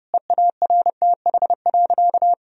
Morse code, heard as a keyed tone, about 32 wpm, 700 Hz. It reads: EARTH.
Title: Sentences from Top 1000 Words - Review Part 1 - 30wpm